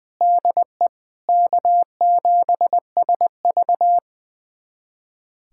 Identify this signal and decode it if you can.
Morse code, heard as a keyed tone, about 20 wpm, 700 Hz. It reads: DE K7SV
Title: USA Callsign DE Callsign - 4 Characters - 20wpm